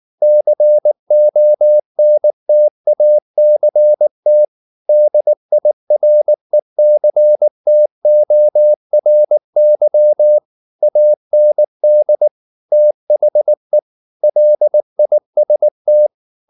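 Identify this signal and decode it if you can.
Morse code, heard as a keyed tone, about 19 wpm, 600 Hz. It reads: CONTACT DIRECTORY AND THE LIST
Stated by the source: Morse code at 19 wpm from CW med Gnister, reading Ken Follett's novel Never in English.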